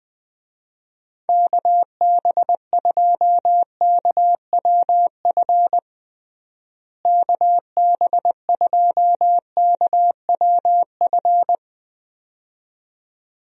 Morse code, heard as a keyed tone, about 20 wpm, 700 Hz. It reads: KB2KWF KB2KWF
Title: USA Call Signs - 6 Characters - Repeated 2x - 20wpm